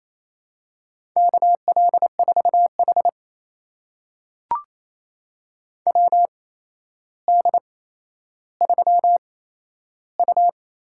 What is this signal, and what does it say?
Morse code, heard as a keyed tone, about 28 wpm, spaced as for 12 wpm, 700 Hz. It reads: KL4H WD3U
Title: USA Call Signs - 4 Characters - 8x Character Spacing - 28wpm